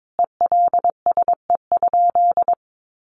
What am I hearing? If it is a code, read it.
Morse code, heard as a keyed tone, about 22 wpm, 700 Hz. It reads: ELSE?
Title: Sentences from Top 900 Words 22wpm